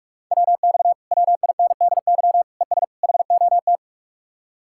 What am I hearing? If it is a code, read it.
Morse code, heard as a keyed tone, about 49 wpm, 700 Hz. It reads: WX WINDY ES HOT